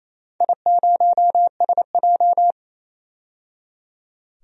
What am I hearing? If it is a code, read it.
Morse code, heard as a keyed tone, about 28 wpm, 700 Hz. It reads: I0SJ
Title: DXCC-100 Call Signs - 4 Characters - 28wpm